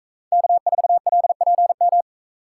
Morse code, heard as a keyed tone, about 40 wpm, 700 Hz. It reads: K4LPM